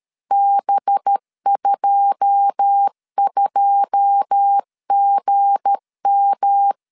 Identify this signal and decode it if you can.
Morse code, heard as a keyed tone, about 13 wpm, 800 Hz. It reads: B22GM
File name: CW / DET / 1 9 1500C S B 12wpm 3819